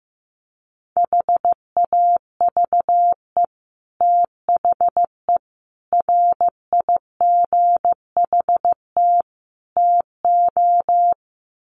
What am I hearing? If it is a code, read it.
Morse code, heard as a keyed tone, about 15 wpm, 700 Hz. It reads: HAVE THE RIGHT TO
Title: Sets of 4 Words 15wpm